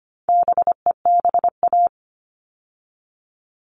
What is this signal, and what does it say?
Morse code, heard as a keyed tone, about 25 wpm, 700 Hz. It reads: BEBA
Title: All Cut Numbers - 4-Digit 25wpm